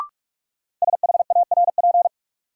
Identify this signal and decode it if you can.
Morse code, heard as a keyed tone, about 41 wpm, 700 Hz. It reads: SHARP
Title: Sentences from Top 1600 Words - New Word Drill Alt - 45wpm